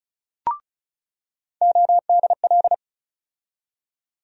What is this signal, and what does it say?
Morse code, heard as a keyed tone, about 36 wpm, 700 Hz. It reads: ODL
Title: USA Call Sign Postfixes - 3 Letters 35wpm